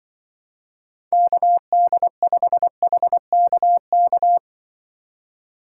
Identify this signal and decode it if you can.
Morse code, heard as a keyed tone, about 24 wpm, 700 Hz. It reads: KD5HKK
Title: Speed Racing - USA 2x3 Call Signs 20wpm